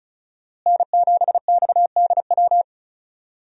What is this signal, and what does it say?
Morse code, heard as a keyed tone, about 36 wpm, 700 Hz. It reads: N7XDW
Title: USA Call Signs - 5 Characters - Repeated 3x - 35wpm